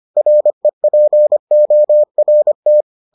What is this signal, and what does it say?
Morse code, heard as a keyed tone, about 25 wpm, 600 Hz. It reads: REPORT